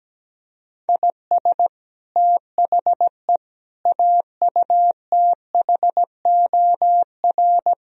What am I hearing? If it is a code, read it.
Morse code, heard as a keyed tone, about 17 wpm, 700 Hz. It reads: IS THE AUTHOR
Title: Speed-Racing - Sets of 4 Words 17wpm